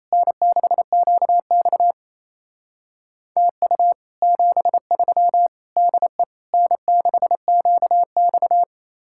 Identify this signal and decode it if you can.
Morse code, heard as a keyed tone, about 36 wpm, 700 Hz. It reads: N6QX TU 73 DE N6QX